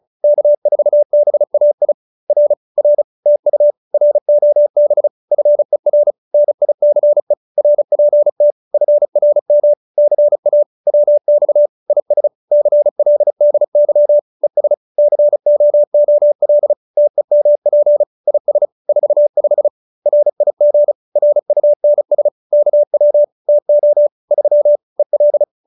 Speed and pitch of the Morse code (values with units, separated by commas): 35 wpm, 590 Hz